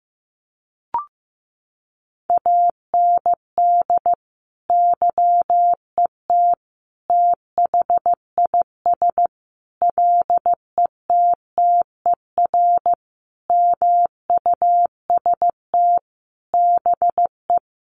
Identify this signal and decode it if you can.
Morse code, heard as a keyed tone, about 15 wpm, 700 Hz. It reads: AND YET THIS LETTER MUST BE
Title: Sentences from Top 1400 Words 15wpm